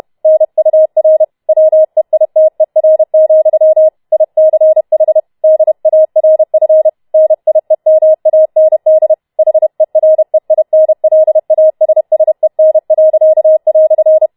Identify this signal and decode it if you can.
Morse code, heard as a keyed tone, about 31 wpm, 630 Hz. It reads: NUR WEITER, ICH DARF NIEMAND HEREINLASSEN."
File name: Schneewittchen Teil4 32wpm